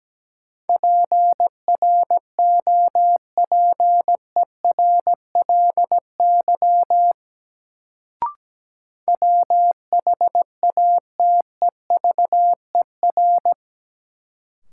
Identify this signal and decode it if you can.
Morse code, heard as a keyed tone, about 17 wpm, 700 Hz. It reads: PROPERLY WHATEVER
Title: Sentences from Top 1100 Words - New Word Drill Alt - 17wpm